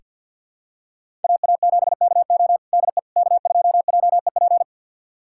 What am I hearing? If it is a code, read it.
Morse code, heard as a keyed tone, about 46 wpm, 700 Hz. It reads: AA7KO DE K2JEP
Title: USA Callsign DE Callsign - 5 Characters - 50wpm